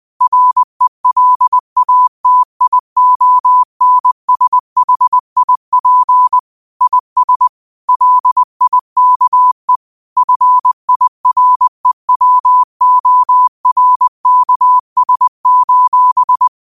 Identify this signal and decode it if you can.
Morse code, heard as a keyed tone, about 20 wpm, 1000 Hz. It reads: RELATIONSHIP IS LIKE FIREWORKS: